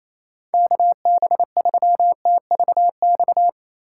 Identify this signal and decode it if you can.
Morse code, heard as a keyed tone, about 28 wpm, 700 Hz. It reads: KB3TVX
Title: USA Call Signs - 3x Character Spacing - 28wpm